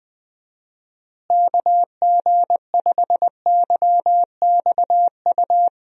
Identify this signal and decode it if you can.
Morse code, heard as a keyed tone, about 20 wpm, 700 Hz. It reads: KG5YXU
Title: USA Call Signs - 6 Characters - Repeated 3x - 20wpm